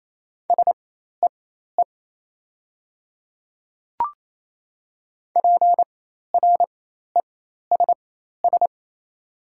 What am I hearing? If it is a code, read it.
Morse code, heard as a keyed tone, about 28 wpm, spaced as for 13 wpm, 690 Hz. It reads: SEE PRESS
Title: 4x Character Spacing - Top 300 Words - 28wpm